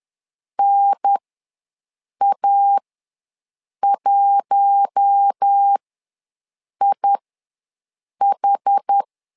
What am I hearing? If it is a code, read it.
Morse code, heard as a keyed tone, about 10 wpm, spaced as for 6 wpm, 800 Hz. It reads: NA1IH
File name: LEZIONI / 4 Q1 ws 10wpm 2538